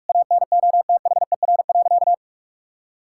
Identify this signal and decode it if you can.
Morse code, heard as a keyed tone, about 46 wpm, 700 Hz. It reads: ANOTHER.
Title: Sentences from Top 1000 Words - Review Part 6 - 45wpm